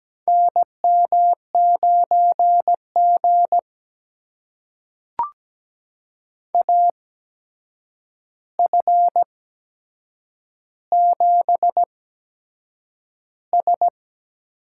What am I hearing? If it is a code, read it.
Morse code, heard as a keyed tone, about 17 wpm, spaced as for 7 wpm, 700 Hz. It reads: NM9G AF7S
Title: USA Call Signs - 4 Characters - 8x Character Spacing - 17wpm